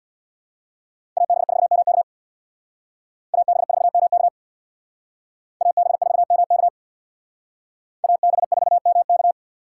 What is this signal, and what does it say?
Morse code, heard as a keyed tone, about 61 wpm, 700 Hz. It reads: AB4MK AB4MK AB4MK AB4MK
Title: Speed Racing - USA Call Signs 50wpm